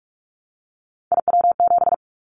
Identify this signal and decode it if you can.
Morse code, heard as a keyed tone, about 44 wpm, 700 Hz. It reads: IW7
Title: Callsign Prefixes for Top 25 DXCC - Repeated 3x - 45wpm